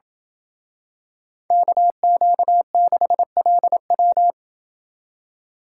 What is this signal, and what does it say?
Morse code, heard as a keyed tone, about 27 wpm, 700 Hz. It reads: KQ6LW